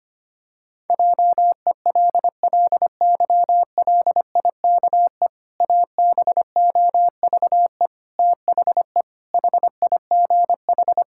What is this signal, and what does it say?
Morse code, heard as a keyed tone, about 25 wpm, 700 Hz. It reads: JELLYLIKE ABOVE THE HIGH
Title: Pangram Challenge v2 - 25wpm